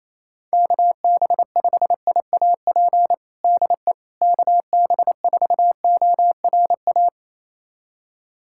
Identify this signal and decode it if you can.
Morse code, heard as a keyed tone, about 28 wpm, 700 Hz. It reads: KB5IAP DE KB4ORA
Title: USA Callsign DE Callsign - 6 Characters - 28wpm